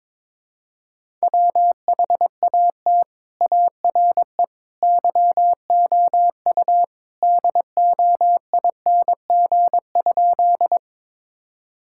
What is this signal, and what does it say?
Morse code, heard as a keyed tone, about 22 wpm, 700 Hz. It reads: WHAT ARE YOU DOING?
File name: practice / 22wpm / Sentences with 3-Word Phrases - 22wpm